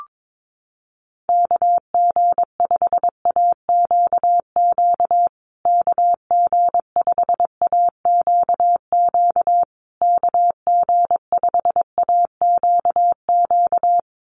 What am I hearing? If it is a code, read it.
Morse code, heard as a keyed tone, about 22 wpm, 700 Hz. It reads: KG5AQQ KG5AQQ KG5AQQ
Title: USA Call Signs Repeated 3x 22wpm